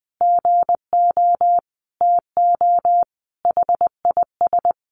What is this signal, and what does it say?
Morse code, heard as a keyed tone, about 20 wpm, 700 Hz. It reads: GO TO HIS
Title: Additional Easy Phrases 20wpm